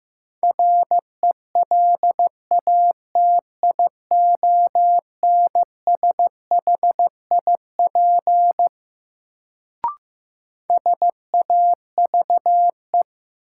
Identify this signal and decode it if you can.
Morse code, heard as a keyed tone, about 15 wpm, 700 Hz. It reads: RELATIONSHIP SAVE
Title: Sentences from Top 1100 Words - New Word Drill Alt - 15wpm